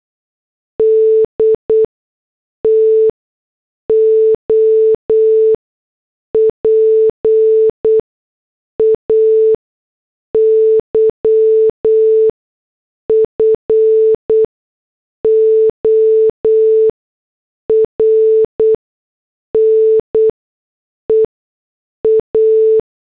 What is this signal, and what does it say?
Morse code, heard as a keyed tone, about 8 wpm, spaced as for 6 wpm, 440 Hz. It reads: D TO PAY FOR NEA